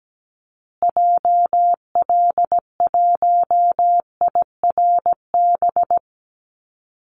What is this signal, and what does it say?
Morse code, heard as a keyed tone, about 17 wpm, 700 Hz. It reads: JL1IRB
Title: Call Signs Repeated 3x - 17wpm